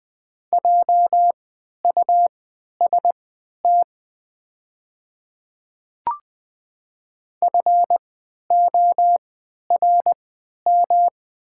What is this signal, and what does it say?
Morse code, heard as a keyed tone, about 20 wpm, spaced as for 11 wpm, 690 Hz. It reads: JUST FORM
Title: 3x Character Spacing - Top 200 Words - 20wpm